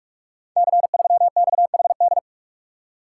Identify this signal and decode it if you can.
Morse code, heard as a keyed tone, about 44 wpm, 690 Hz. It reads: C3XHD
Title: Code Groups - 5 Letter:Numbers - 45wpm